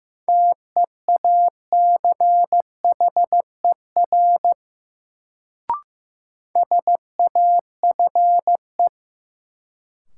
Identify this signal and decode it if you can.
Morse code, heard as a keyed tone, about 15 wpm, 700 Hz. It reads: TEACHER SAFE
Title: Sentences from Top 900 Words - New Word Drill - 15wpm